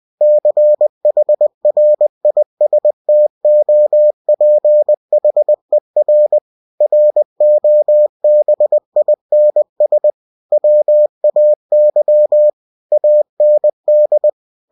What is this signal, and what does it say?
Morse code, heard as a keyed tone, about 20 wpm, 600 Hz. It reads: CHRISTOPHER ROBINS WAY AND
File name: cw-books / winnie-the-pooh / winnie-20wpm-page-0091